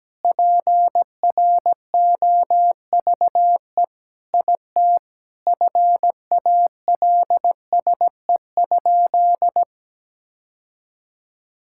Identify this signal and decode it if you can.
Morse code, heard as a keyed tone, about 17 wpm, 700 Hz. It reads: PROVE IT FALSE?
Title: Sentences from Top 1600 Words 17wpm